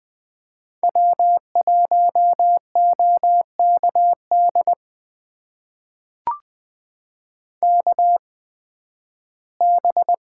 Morse code, heard as a keyed tone, about 20 wpm, spaced as for 13 wpm, 700 Hz. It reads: W1OKD KB
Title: USA Call Signs - 5 Characters - 8x Character Spacing - 20wpm